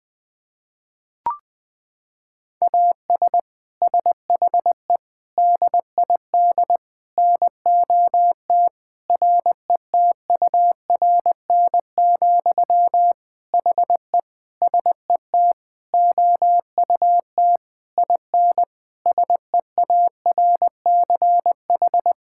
Morse code, heard as a keyed tone, about 21 wpm, 700 Hz. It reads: AS SHE DID NOT RETURN, HE SET OUT IN SEARCH
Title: Sentences from Top 1000 Words - Review Part 5 - 20wpm